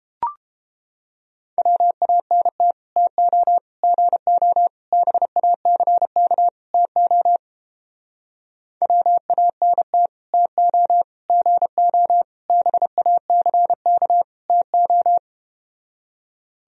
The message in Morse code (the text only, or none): WANT TO GO BACK TO WANT TO GO BACK TO